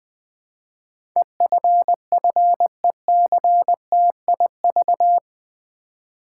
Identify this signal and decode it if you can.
Morse code, heard as a keyed tone, about 20 wpm, 700 Hz. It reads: EFFECTIV
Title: Build A Word - 4 Letters At A Time - 11-Letter Words - 20wpm